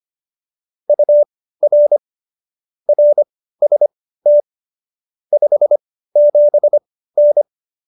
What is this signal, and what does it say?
Morse code, heard as a keyed tone, about 25 wpm, spaced as for 15 wpm, 590 Hz. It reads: UR RST 57N